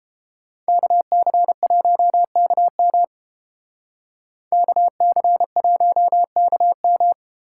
Morse code, heard as a keyed tone, about 32 wpm, 700 Hz. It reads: KC1KM KC1KM